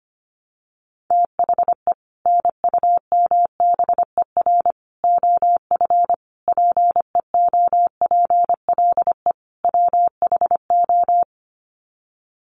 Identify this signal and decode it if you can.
Morse code, heard as a keyed tone, about 25 wpm, 700 Hz. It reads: THE NUMBER OF PEOPLE WHO
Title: Sets of 5 Words Encore 25wpm